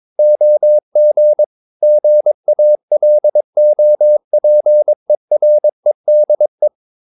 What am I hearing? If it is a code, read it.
Morse code, heard as a keyed tone, about 22 wpm, 600 Hz. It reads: OG GALOPEREDE
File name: Ferie QRS 22wpm